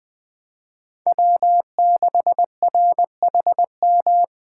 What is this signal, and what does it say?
Morse code, heard as a keyed tone, about 22 wpm, 700 Hz. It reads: W6RHM